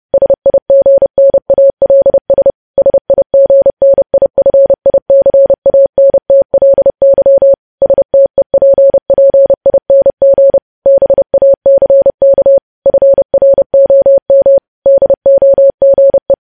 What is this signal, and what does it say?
Morse code, heard as a keyed tone, about 30 wpm, 550 Hz. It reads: SIGNALS SIGNIFICANTLY STEPPING BACK FROM DOGE
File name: podcast / News headlines-1x-30x30